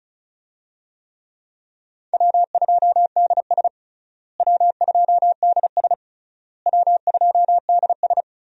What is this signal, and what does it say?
Morse code, heard as a keyed tone, about 34 wpm, 700 Hz. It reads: W2DS W2DS W2DS
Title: USA Call Signs - 4 Characters - Repeated 3x - 35wpm